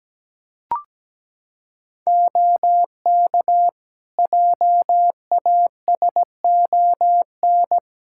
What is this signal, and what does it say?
Morse code, heard as a keyed tone, about 17 wpm, 700 Hz. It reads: OK JASON
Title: Practice QSO - Exchange 4 - Ending - 17wpm